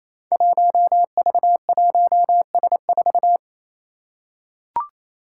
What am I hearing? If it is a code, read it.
Morse code, heard as a keyed tone, about 28 wpm, 700 Hz. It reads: 1V1S4
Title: Five Challenging Letter-Number Combinations 28wpm